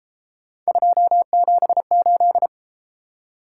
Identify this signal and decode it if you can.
Morse code, heard as a keyed tone, about 33 wpm, 700 Hz. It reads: 278